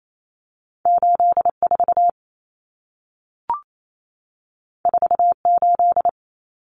Morse code, heard as a keyed tone, about 28 wpm, 700 Hz. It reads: 84 48
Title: Common Cut Numbers - 2-Digit 28wpm